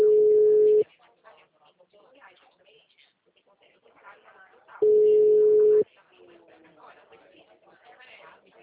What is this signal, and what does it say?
Ringback tone (Europe, 425 Hz), with speech.